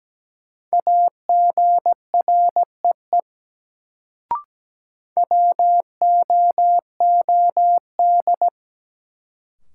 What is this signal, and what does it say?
Morse code, heard as a keyed tone, about 17 wpm, 700 Hz. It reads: AGREE WOOD
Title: Sentences from Top 1100 Words - New Word Drill Alt - 17wpm